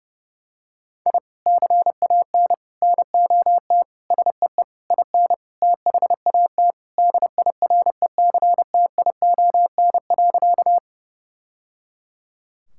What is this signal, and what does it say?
Morse code, heard as a keyed tone, about 30 wpm, 700 Hz. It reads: I CAN NOT SEE IN THAT DIRECTION.
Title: Sentences from Top 1000 Words - Review Part 5 - 30wpm